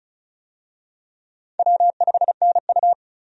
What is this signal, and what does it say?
Morse code, heard as a keyed tone, about 36 wpm, 700 Hz. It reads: W5NU
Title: USA Call Signs - 4 Characters - Repeated 2x - 35wpm